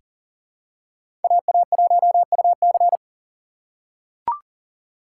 Morse code, heard as a keyed tone, about 39 wpm, 700 Hz. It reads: AA1UC
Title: K1USN SST QSO Practice - 40wpm